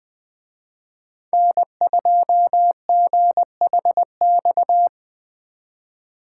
Morse code, heard as a keyed tone, about 20 wpm, 700 Hz. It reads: N2GHX
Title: Speed Racing - USA Call Signs 17wpm